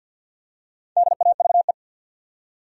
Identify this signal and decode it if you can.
Morse code, heard as a keyed tone, about 53 wpm, 700 Hz. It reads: DAVE
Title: USA Common Names - Max 4 Letters - 50wpm